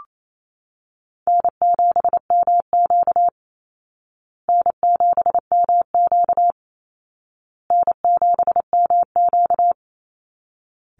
Morse code, heard as a keyed tone, about 28 wpm, 700 Hz. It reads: N7MQ N7MQ N7MQ
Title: Easy Call Signs Repeated 3x - 28wpm